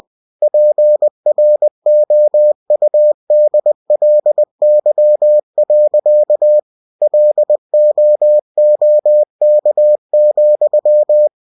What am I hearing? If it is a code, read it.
Morse code, heard as a keyed tone, about 20 wpm, 600 Hz. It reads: PROUDLY. LOOK,
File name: cw-books / winnie-the-pooh / winnie-20wpm-page-0058